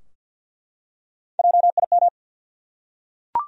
Morse code, heard as a keyed tone, about 50 wpm, 700 Hz. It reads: JIM
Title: USA Common Names - 50wpm